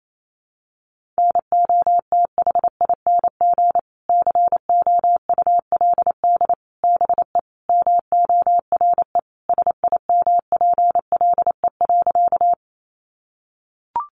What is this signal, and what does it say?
Morse code, heard as a keyed tone, about 28 wpm, 700 Hz. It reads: NOTHING COULD BE MORE SIMPLE.
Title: Sentences from Top 500 Words 28wpm